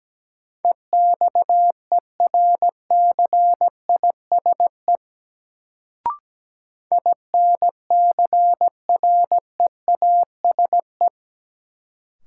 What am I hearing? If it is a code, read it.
Morse code, heard as a keyed tone, about 17 wpm, 700 Hz. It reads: EXERCISE INCREASE
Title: Sentences from Top 1000 Words - Review Part 4 - New Word Drill - 17wpm